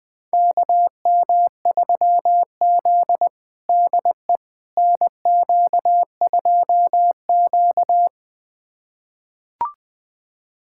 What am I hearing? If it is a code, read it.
Morse code, heard as a keyed tone, about 21 wpm, 700 Hz. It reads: KM3Z DE NQ2Q
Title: USA Callsign DE Callsign - 4 Characters - 20wpm